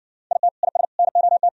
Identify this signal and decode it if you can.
Morse code, heard as a keyed tone, about 60 wpm, 700 Hz. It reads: IT IS NOT